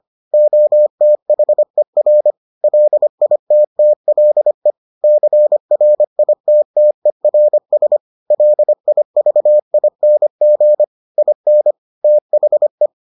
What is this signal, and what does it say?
Morse code, heard as a keyed tone, about 25 wpm, 600 Hz. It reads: OTHER LITTLE CRITTERS LIVING IN THE